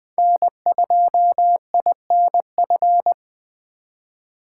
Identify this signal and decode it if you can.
Morse code, heard as a keyed tone, about 20 wpm, 700 Hz. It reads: N2INF
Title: USA Call Signs - 5 Characters - 3x Character Spacing - 20wpm